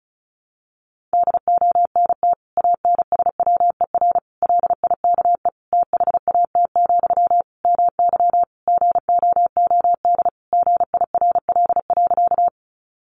Morse code, heard as a keyed tone, about 34 wpm, 700 Hz. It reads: DONT ANSWER LIKE THAT, MY GOOD GIRL.
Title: Sentences from Top 400 Words 35wpm